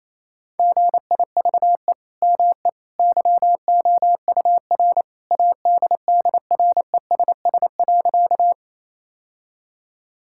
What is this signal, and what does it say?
Morse code, heard as a keyed tone, about 28 wpm, 700 Hz. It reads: GIVE ME YOUR ADDRESS.
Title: Sentences from Top 1000 Words - Review Part 6 - 28wpm